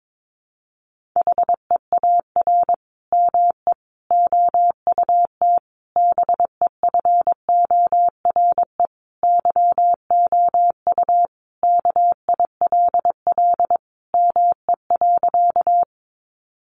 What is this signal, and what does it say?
Morse code, heard as a keyed tone, about 22 wpm, 700 Hz. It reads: HEAR ME OUT BEFORE YOU KILL ME.
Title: Sentences from Top 1400 Words 22wpm